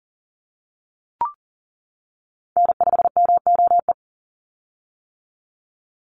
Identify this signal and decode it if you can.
Morse code, heard as a keyed tone, about 40 wpm, 700 Hz. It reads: N5MOE
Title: USA Call Signs 40wpm